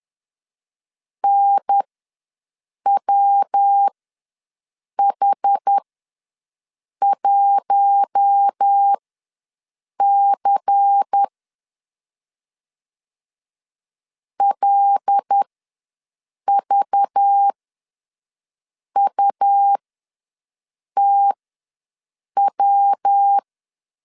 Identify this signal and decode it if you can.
Morse code, heard as a keyed tone, about 10 wpm, spaced as for 6 wpm, 800 Hz. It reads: NWH1C LVUTW
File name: LEZIONI / 8 YJ45 / 8 YJ4 ws 10wpm 4020